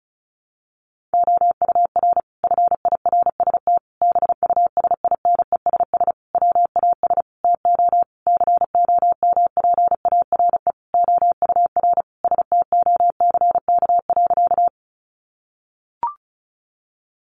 Morse code, heard as a keyed tone, about 34 wpm, 700 Hz. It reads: OUR FIRST BUSINESS WAS TO COMPARE OUR STOCK.
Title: Sentences from Top 1300 Words 35wpm